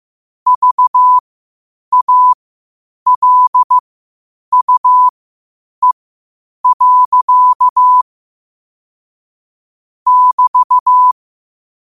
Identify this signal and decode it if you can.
Morse code, heard as a keyed tone, about 15 wpm, spaced as for 8 wpm, 1000 Hz. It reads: VALUE. <BT>
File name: qotdfiles / 8wpm